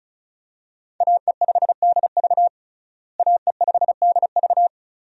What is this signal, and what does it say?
Morse code, heard as a keyed tone, about 34 wpm, 700 Hz. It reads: AE5DV AE5DV
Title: USA Call Signs - 5 Characters - Repeated 2x - 35wpm